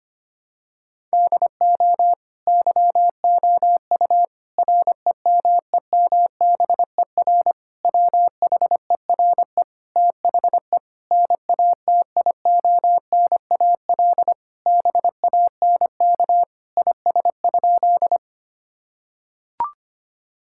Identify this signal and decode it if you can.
Morse code, heard as a keyed tone, about 25 wpm, 700 Hz. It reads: DO YOU REMEMBER WHERE THE NATIONAL BANK IS?
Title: Sentences from Top 1000 Words - Review Part 6 - 25wpm